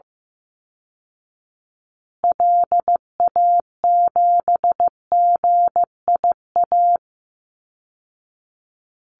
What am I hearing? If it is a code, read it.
Morse code, heard as a keyed tone, about 15 wpm, 700 Hz. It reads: LA7GIA
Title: Call Signs Repeated 2x - 15wpm